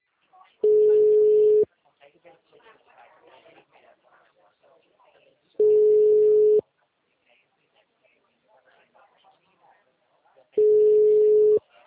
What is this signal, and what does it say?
Ringback tone (Europe, 425 Hz), with speech.
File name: out- -2009